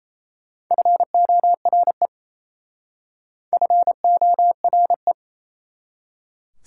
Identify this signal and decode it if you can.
Morse code, heard as a keyed tone, about 38 wpm, 690 Hz. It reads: FORE FORE